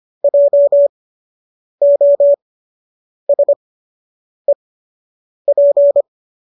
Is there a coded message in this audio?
Yes: Morse code from 0.2 to 6.0 s, heard as a keyed tone, about 25 wpm, spaced as for 8 wpm, 570 Hz. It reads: JOSEP